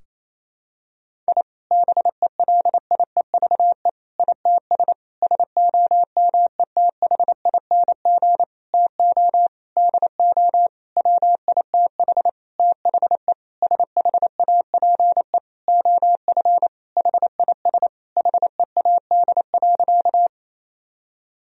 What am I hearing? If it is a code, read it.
Morse code, heard as a keyed tone, about 28 wpm, 700 Hz. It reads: I BELIEVE ITS SOMETHING TO DO WITH THE SHAPE OF HIS HEAD.
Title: Sentences from Top 1000 Words - Review Part 3 - 28wpm